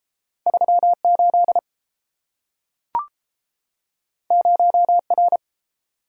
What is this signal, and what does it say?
Morse code, heard as a keyed tone, about 30 wpm, 700 Hz. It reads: 38 0R